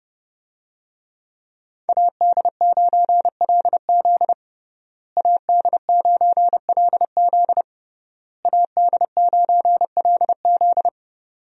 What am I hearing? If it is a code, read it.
Morse code, heard as a keyed tone, about 32 wpm, 700 Hz. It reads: AD9LZ AD9LZ AD9LZ
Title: USA Call Signs - 5 Characters - Repeated 3x - 30wpm